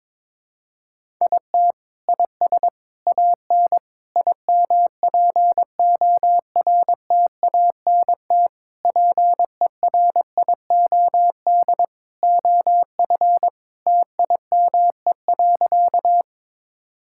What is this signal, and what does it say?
Morse code, heard as a keyed tone, about 22 wpm, 700 Hz. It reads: IT IS AN IMPORTANT PERIOD OF TIME.
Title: Sentences from Top 1000 Words - Review Part 5 - 22wpm